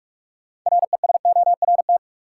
Morse code, heard as a keyed tone, about 45 wpm, 700 Hz. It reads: RESORT
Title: Six Letter Words - 45wpm